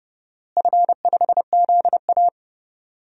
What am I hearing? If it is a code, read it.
Morse code, heard as a keyed tone, about 30 wpm, 700 Hz. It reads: F5ZA
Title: DXCC-100 Call Signs - 4 Characters - Repeated 3x - 30wpm